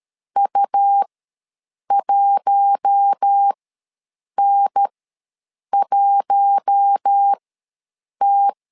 Morse code, heard as a keyed tone, about 13 wpm, spaced as for 7 wpm, 800 Hz. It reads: U1N1T
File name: LEZIONI / 4 Q1 ws 12wpm 2129